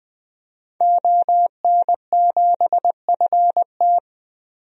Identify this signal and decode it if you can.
Morse code, heard as a keyed tone, about 20 wpm, 700 Hz. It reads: ON7FT
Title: DXCC-100 Call Signs - 5 Characters - 20wpm